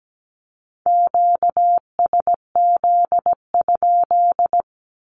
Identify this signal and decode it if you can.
Morse code, heard as a keyed tone, about 17 wpm, 700 Hz. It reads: QSZ?
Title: Q-Codes with Spoken Meaning - 17wpm